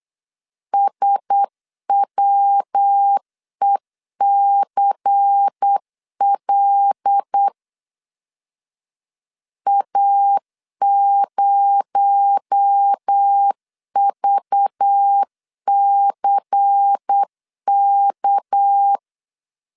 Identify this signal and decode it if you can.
Morse code, heard as a keyed tone, about 8 wpm, 800 Hz. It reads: SWECL A0VCK
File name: LEZIONI / 6 PKX0 / 6 PKX0 8wpm 3341